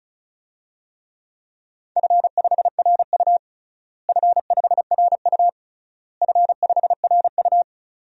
Morse code, heard as a keyed tone, about 36 wpm, 700 Hz. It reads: F5RU F5RU F5RU
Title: DXCC-100 Call Signs - 4 Characters - Repeated 3x - 35wpm